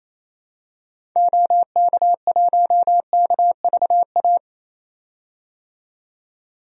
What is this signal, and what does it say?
Morse code, heard as a keyed tone, about 28 wpm, 700 Hz. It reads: OK1KVA
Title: DXCC-100 Call Signs - 6 Characters - 28wpm